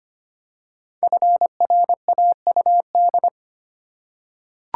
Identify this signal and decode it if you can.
Morse code, heard as a keyed tone, about 25 wpm, 700 Hz. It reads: FRAUD
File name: 5 Letter Words - First 3 Letters Given - 25wpm